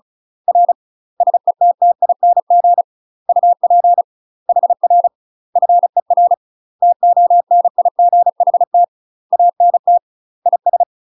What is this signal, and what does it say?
Morse code, heard as a keyed tone, about 36 wpm, 700 Hz. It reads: R SETTING UP HR FER TONIGHT ANT IS
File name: UK MASTER QSOs TRAINING / 35WPM / UK MASTER QSOs 0014